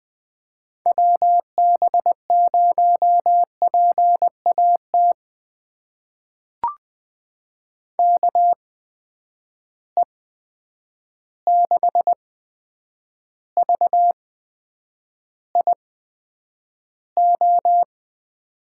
Morse code, heard as a keyed tone, about 20 wpm, spaced as for 9 wpm, 700 Hz. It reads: WB0PAT KE6VIO
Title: USA Call Signs - 8x Character Spacing - 20wpm